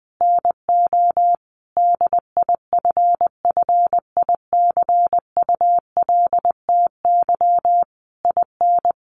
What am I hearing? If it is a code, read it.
Morse code, heard as a keyed tone, about 20 wpm, 700 Hz. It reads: NO DIFFICULTY IN
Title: Sentences from Top 1500 Words 20wpm